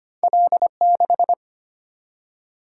Morse code, heard as a keyed tone, about 25 wpm, 700 Hz. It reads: L6
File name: Two Letter-Number Combinations - 25wpm